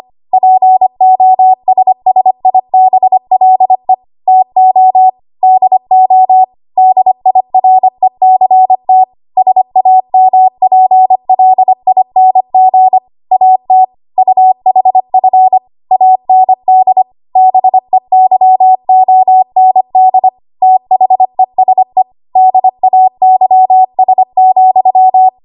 Morse code, heard as a keyed tone, about 25 wpm, 750 Hz. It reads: POSSIBLE TO DO DIRECT SAMPLING AT UHF AND BEYOND THESE DAYS,
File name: Archive / 25 WPM / 25WPM